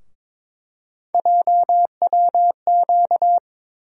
Morse code, heard as a keyed tone, about 22 wpm, 700 Hz. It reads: JWQ
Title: USA Call Sign Postfixes - 3 Letters 22wpm